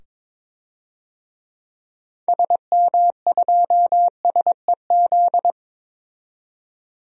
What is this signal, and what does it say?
Morse code, heard as a keyed tone, about 21 wpm, 700 Hz. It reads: SM2SEZ
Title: DXCC-100 Call Signs - 6 Characters - 22wpm